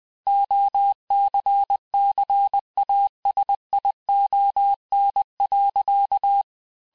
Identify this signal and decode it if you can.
Morse code, heard as a keyed tone, about 20 wpm, 800 Hz. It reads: OCCASION.
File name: cours-du-mercredi / audios / cours-870-texte-CLAIR-vitesse-19-MPM